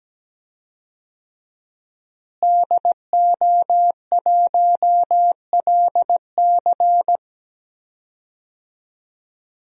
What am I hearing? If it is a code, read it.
Morse code, heard as a keyed tone, about 17 wpm, 700 Hz. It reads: DO1LC
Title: DXCC-100 Call Signs - 5 Characters - Repeated 2x - 17wpm